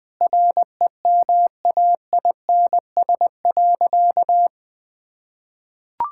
Morse code, heard as a keyed tone, about 20 wpm, 700 Hz. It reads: REMAINS.
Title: Sentences with 2-Word Phrases - 20wpm